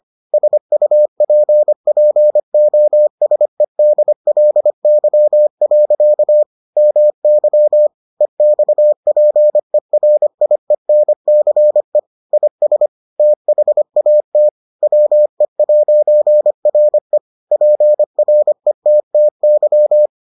Morse code, heard as a keyed tone, about 25 wpm, 600 Hz. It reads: SUPPOSEDLY. MY EXPERIENCE IS THAT WE'RE PRETTY